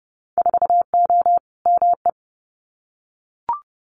Morse code, heard as a keyed tone, about 30 wpm, 700 Hz. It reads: 4O ME
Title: Winter Field Day Exchanges Encore 30wpm